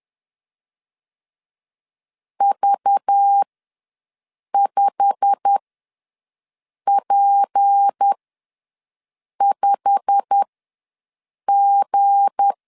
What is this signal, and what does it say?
Morse code, heard as a keyed tone, about 10 wpm, spaced as for 5 wpm, 800 Hz. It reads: V5P5G
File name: LEZIONI / 8 YJ45 / 8 YJ45 ws 10wpm 3843